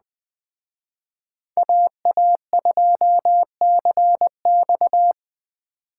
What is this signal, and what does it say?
Morse code, heard as a keyed tone, about 20 wpm, 700 Hz. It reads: AA2CX
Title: Speed Racing - USA 2x2 Call Signs 17wpm